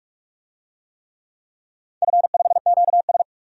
Morse code, heard as a keyed tone, about 44 wpm, 700 Hz. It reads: F5QS
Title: DXCC-100 Call Signs - 4 Characters - 45wpm